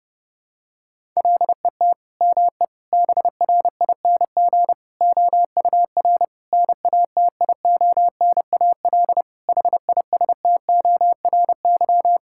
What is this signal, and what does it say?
Morse code, heard as a keyed tone, about 32 wpm, 700 Hz. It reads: LET ME BRING OUR NATIONAL HISTORY
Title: Sentences from Top 1000 Words - Review Part 6 - 30wpm